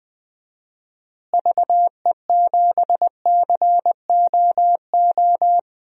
Morse code, heard as a keyed tone, about 20 wpm, 700 Hz. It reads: VE7COO
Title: DXCC-100 Call Signs - 6 Characters - Repeated 3x - 20wpm